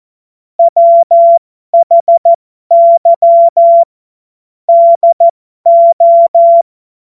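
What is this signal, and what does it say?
Morse code, heard as a keyed tone, about 14 wpm, spaced as for 12 wpm, 680 Hz. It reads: WHY DO
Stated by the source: Morse code joke.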